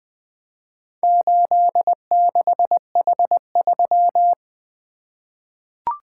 Morse code, heard as a keyed tone, about 20 wpm, 700 Hz. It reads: 86H3
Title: Four Letter-Number Combinations - 20wpm